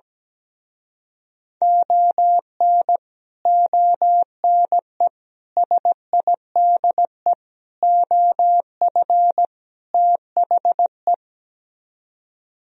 Morse code, heard as a keyed tone, about 17 wpm, 700 Hz. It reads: ON ONE SIDE OF THE
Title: Speed-Racing - Sets of 5 Words 17wpm